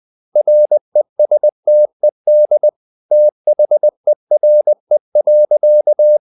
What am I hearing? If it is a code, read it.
Morse code, heard as a keyed tone, about 20 wpm, 600 Hz. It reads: RESTED THERE.
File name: the old man and the sea / the old man and the sea part 1 20wpm